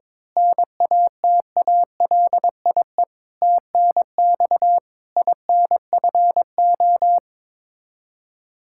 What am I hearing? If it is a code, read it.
Morse code, heard as a keyed tone, about 22 wpm, 700 Hz. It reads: NATALIE TNX INFO
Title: Practice QSO - Exchange 3 - More Info - 22wpm